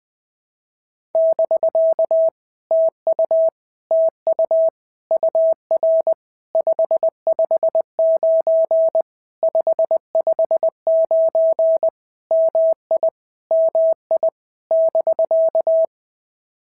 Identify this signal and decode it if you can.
Morse code, heard as a keyed tone, about 20 wpm, 650 Hz. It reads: <BK> TU TU UR 559 559 MI MI <BK>
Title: POTA QSO Practice - 20wpm